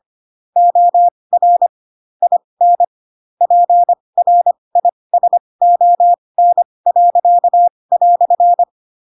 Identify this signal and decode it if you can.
Morse code, heard as a keyed tone, about 25 wpm, spaced as for 20 wpm, 700 Hz. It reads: OR IN PRISON."
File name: Pinocchio / Pinocchio ch4 15 25 ik1hge